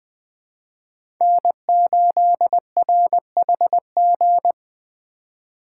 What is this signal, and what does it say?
Morse code, heard as a keyed tone, about 21 wpm, 700 Hz. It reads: N8RHG
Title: USA Call Signs - 5 Characters - 2x Character Spacing - 20wpm